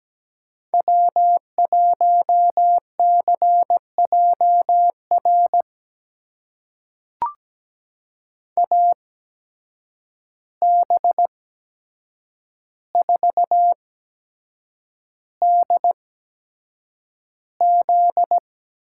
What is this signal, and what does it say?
Morse code, heard as a keyed tone, about 17 wpm, spaced as for 7 wpm, 700 Hz. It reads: W1CJR AB4DZ
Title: USA Call Signs - 5 Characters - 8x Character Spacing - 17wpm